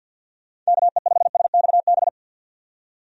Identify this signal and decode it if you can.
Morse code, heard as a keyed tone, about 47 wpm, 700 Hz. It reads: KE5SXB
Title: USA Call Signs - 6 Characters - 50wpm